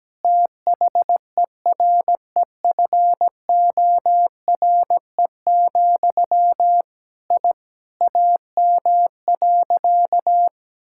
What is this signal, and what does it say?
Morse code, heard as a keyed tone, about 17 wpm, 700 Hz. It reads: THEREFORE, I AM.
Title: Sentences from Top 1000 Words - Review Part 5 - 17wpm